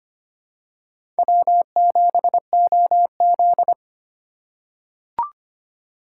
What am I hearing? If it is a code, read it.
Morse code, heard as a keyed tone, about 25 wpm, 700 Hz. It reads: W7OZ
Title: USA Call Signs - 2x Character Spacing - 25wpm